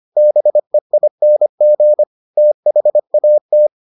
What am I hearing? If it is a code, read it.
Morse code, heard as a keyed tone, about 25 wpm, 600 Hz. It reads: BEING THAT